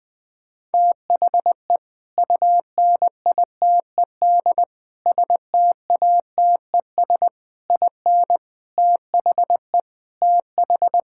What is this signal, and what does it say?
Morse code, heard as a keyed tone, about 21 wpm, 700 Hz. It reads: THE UNITED STATES IN THE TH
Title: Sentences with 5-Word Phrases - 20wpm